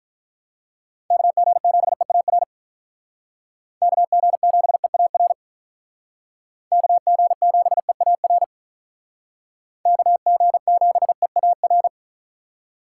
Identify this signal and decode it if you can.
Morse code, heard as a keyed tone, about 46 wpm, 700 Hz. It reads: KG7EAR KG7EAR KG7EAR KG7EAR